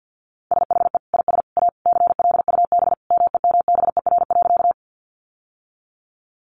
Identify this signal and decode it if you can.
Morse code, heard as a keyed tone, about 50 wpm, 700 Hz. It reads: SHE IS A CLUB MEMBER.
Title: Sentences from Top 1200 Words 50wpm